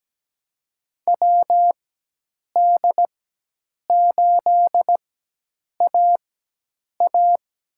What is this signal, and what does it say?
Morse code, heard as a keyed tone, about 17 wpm, spaced as for 8 wpm, 700 Hz. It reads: WD8AA
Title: USA Call Signs - 6 Characters - 4x Character Spacing - 17wpm